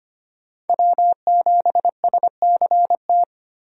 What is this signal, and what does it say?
Morse code, heard as a keyed tone, about 25 wpm, 700 Hz. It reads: W7SCT
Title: USA Call Signs - 5 Characters - Repeated 2x - 25wpm